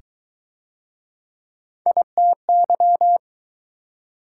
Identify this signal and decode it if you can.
Morse code, heard as a keyed tone, about 19 wpm, 700 Hz. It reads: ITY